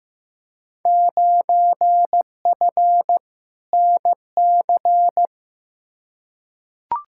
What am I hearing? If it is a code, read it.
Morse code, heard as a keyed tone, about 15 wpm, 700 Hz. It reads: 9F NC
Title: Field Day Exchanges Encore - 15wpm